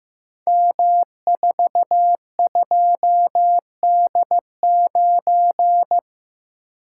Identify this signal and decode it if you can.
Morse code, heard as a keyed tone, about 15 wpm, 700 Hz. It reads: M42D9
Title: Code Groups - 5 Letter:Numbers - 15wpm